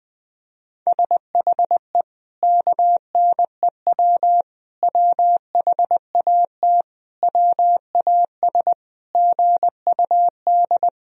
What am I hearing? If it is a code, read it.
Morse code, heard as a keyed tone, about 20 wpm, 700 Hz. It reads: SHE KNEW WHAT WAS GUD
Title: Sentences from Top 400 Words and Abbreviations 20wpm